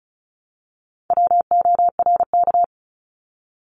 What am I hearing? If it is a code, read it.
Morse code, heard as a keyed tone, about 36 wpm, 700 Hz. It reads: WORK
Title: Multi-letter Phonemes 35wpm